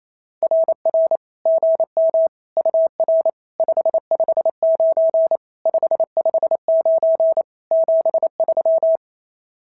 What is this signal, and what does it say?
Morse code, heard as a keyed tone, about 29 wpm, 590 Hz. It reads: RR GM UR 559 559 73
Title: SOTA QSO Practice 28wpm